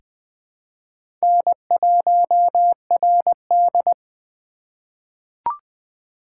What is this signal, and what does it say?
Morse code, heard as a keyed tone, about 20 wpm, 700 Hz. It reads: N1RD
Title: USA Call Signs - 4 Characters - 2x Character Spacing - 20wpm